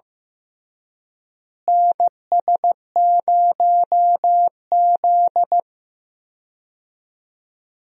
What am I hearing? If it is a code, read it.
Morse code, heard as a keyed tone, about 15 wpm, 700 Hz. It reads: NS0Z
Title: USA Call Signs - 4 Characters - Repeated 3x - 15wpm